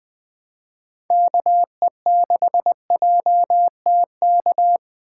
Morse code, heard as a keyed tone, about 20 wpm, 700 Hz. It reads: KE6JTK
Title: USA Call Signs - 2x Character Spacing - 20wpm